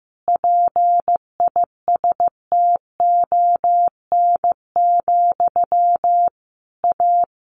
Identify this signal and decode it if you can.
Morse code, heard as a keyed tone, about 15 wpm, 700 Hz. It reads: PISTON, A
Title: A Sound Of Thunder 15wpm